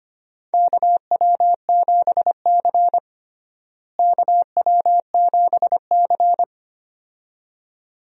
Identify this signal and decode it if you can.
Morse code, heard as a keyed tone, about 25 wpm, 700 Hz. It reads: KW7C KW7C
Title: USA Call Signs - 4 Characters - Repeated 2x - 25wpm